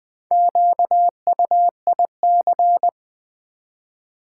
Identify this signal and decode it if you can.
Morse code, heard as a keyed tone, about 20 wpm, 700 Hz. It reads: QUIC
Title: Build A Word - 1 Letter At A Time - 5-Letter Words - 20wpm